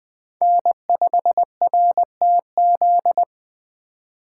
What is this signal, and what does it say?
Morse code, heard as a keyed tone, about 20 wpm, 700 Hz. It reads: N5RTZ
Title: USA Call Signs - 2x Character Spacing - 20wpm